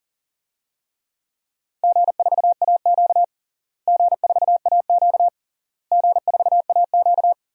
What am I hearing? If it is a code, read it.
Morse code, heard as a keyed tone, about 37 wpm, 700 Hz. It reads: G4AQ G4AQ G4AQ
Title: DXCC-100 Call Signs - 4 Characters - Repeated 3x - 40wpm